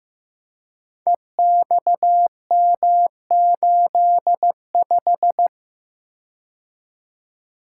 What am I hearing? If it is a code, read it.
Morse code, heard as a keyed tone, about 15 wpm, 700 Hz. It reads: EXM85
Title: Code Groups - 5 Letter:Numbers - 15wpm